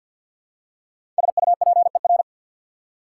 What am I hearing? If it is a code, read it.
Morse code, heard as a keyed tone, about 50 wpm, 700 Hz. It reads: SUPER
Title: Prefixes and Suffixes 50wpm